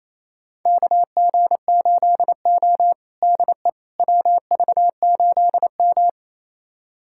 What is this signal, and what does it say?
Morse code, heard as a keyed tone, about 28 wpm, 700 Hz. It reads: KG8O DE WV8M
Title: USA Callsign DE Callsign - 4 Characters - 28wpm